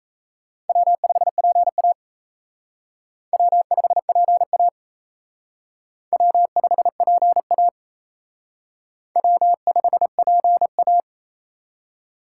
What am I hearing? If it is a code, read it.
Morse code, heard as a keyed tone, about 34 wpm, 700 Hz. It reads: W5PA W5PA W5PA W5PA
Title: Speed Racing - USA 1x2 Call Signs 28wpm